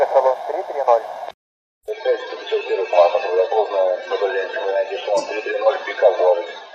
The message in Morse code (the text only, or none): none